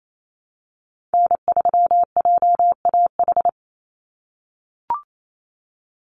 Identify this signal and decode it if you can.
Morse code, heard as a keyed tone, about 28 wpm, 700 Hz. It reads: N3JAH
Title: USA Call Signs 28wpm